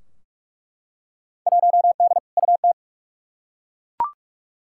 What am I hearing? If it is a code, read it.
Morse code, heard as a keyed tone, about 47 wpm, 700 Hz. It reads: 1D UT
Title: Field Day Exchanges - 45wpm